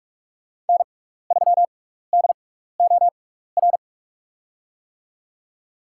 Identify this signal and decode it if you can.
Morse code, heard as a keyed tone, about 47 wpm, spaced as for 16 wpm, 700 Hz. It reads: N3DOR
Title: USA Call Signs - 5 Characters - 6x Character Spacing - 45wpm